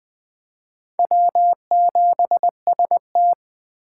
Morse code, heard as a keyed tone, about 21 wpm, 700 Hz. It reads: W7ST